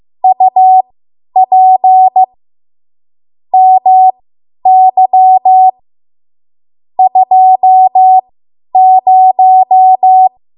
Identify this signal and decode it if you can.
Morse code, heard as a keyed tone, about 15 wpm, spaced as for 10 wpm, 750 Hz. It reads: UP MY 20
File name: Archive / 10 WPM / 10WPM